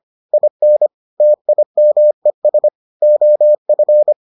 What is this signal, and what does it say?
Morse code, heard as a keyed tone, about 25 wpm, 600 Hz. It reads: IN TIMES OF